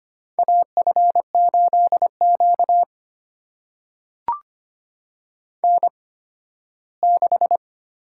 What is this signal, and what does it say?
Morse code, heard as a keyed tone, about 25 wpm, spaced as for 15 wpm, 700 Hz. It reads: AF8Q N6
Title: USA Call Signs - 8x Character Spacing - 25wpm